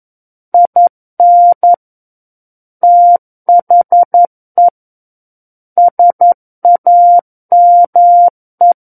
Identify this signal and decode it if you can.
Morse code, heard as a keyed tone, about 11 wpm, 700 Hz. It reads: IN THE SAME
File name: Book One / 02-Chapter 2 - 11 WPM